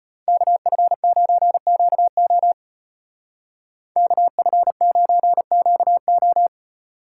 Morse code, heard as a keyed tone, about 32 wpm, 700 Hz. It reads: KF9QO KF9QO